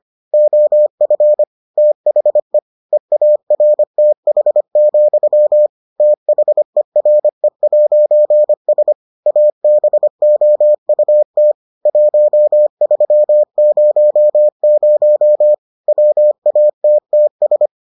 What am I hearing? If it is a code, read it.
Morse code, heard as a keyed tone, about 25 wpm, 600 Hz. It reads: OF THE EARTH, THERE'S ABOUT 1300 WATTS